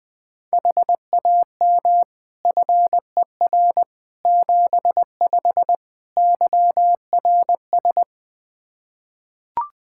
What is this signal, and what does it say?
Morse code, heard as a keyed tone, about 20 wpm, 700 Hz. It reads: HAM FER 75 YRS
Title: Practice QSO - Exchange 3 - More Info - 20wpm